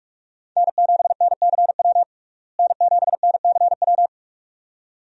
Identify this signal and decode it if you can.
Morse code, heard as a keyed tone, about 44 wpm, 700 Hz. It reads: N7NCW N7NCW
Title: USA Call Signs - 5 Characters - Repeated 2x - 45wpm